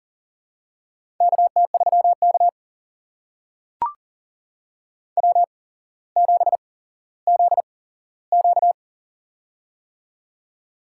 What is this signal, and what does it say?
Morse code, heard as a keyed tone, about 42 wpm, spaced as for 17 wpm, 700 Hz. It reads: KT3K W7ZQ
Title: USA Call Signs - 4 Characters - 8x Character Spacing - 40wpm